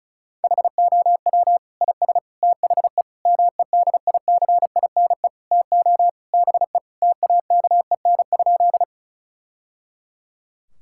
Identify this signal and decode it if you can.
Morse code, heard as a keyed tone, about 34 wpm, 700 Hz. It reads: HOW IS THE MEDICINE TO BE TAKEN?
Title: Sentences from Top 1000 Words - Review Part 6 - 35wpm